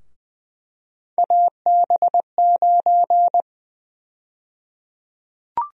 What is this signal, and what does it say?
Morse code, heard as a keyed tone, about 21 wpm, 700 Hz. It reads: AB9
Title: USA Call Sign Prefixes - Repeated 2x - 20wpm